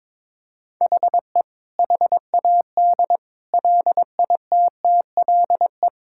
Morse code, heard as a keyed tone, about 22 wpm, 700 Hz. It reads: HE HAD LITTLE
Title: Sentences from Top 1600 Words 22wpm